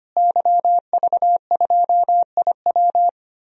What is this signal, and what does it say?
Morse code, heard as a keyed tone, about 25 wpm, 700 Hz. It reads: YV2IW
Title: DXCC-100 Call Signs - 5 Characters - Repeated 2x - 25wpm